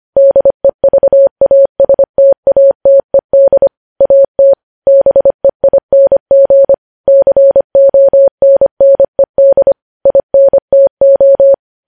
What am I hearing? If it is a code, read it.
Morse code, heard as a keyed tone, about 25 wpm, 550 Hz. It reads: DEVASTATED AT BEING CONNED INTO
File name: podcast / News headlines-1x-25x25